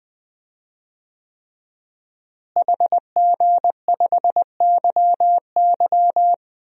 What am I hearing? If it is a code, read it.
Morse code, heard as a keyed tone, about 20 wpm, 700 Hz. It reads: HG5YY
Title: DXCC-100 Call Signs - 5 Characters - Repeated 3x - 20wpm